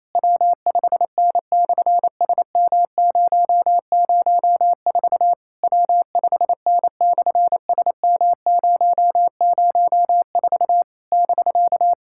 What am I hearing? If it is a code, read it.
Morse code, heard as a keyed tone, about 28 wpm, 700 Hz. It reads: W5N/SM004 W5N/SM004 <BK>
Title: SOTA QSO Practice 28wpm